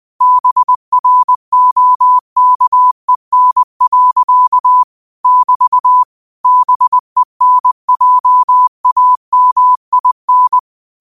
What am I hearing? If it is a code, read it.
Morse code, heard as a keyed tone, about 20 wpm, 1000 Hz. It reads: BROKEN. <BT> BENJAMIN